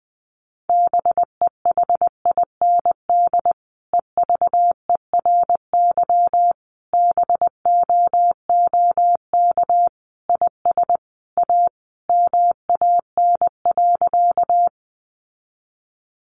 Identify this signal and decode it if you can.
Morse code, heard as a keyed tone, about 20 wpm, 700 Hz. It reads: BEHIND EVERY BOOK IS A MAN.
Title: Sentences from Top 500 Words 20wpm